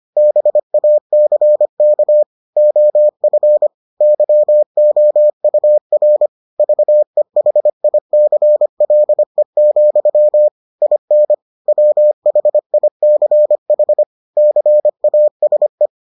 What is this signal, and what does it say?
Morse code, heard as a keyed tone, about 25 wpm, 600 Hz. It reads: BACK OF YOUR VEHICLE, IN WHICH CASE